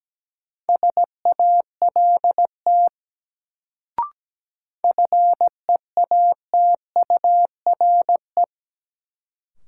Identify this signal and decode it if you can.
Morse code, heard as a keyed tone, about 17 wpm, 700 Hz. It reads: SALT FEATURE
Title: Sentences from Top 800 Words - New Word Drill - 17wpm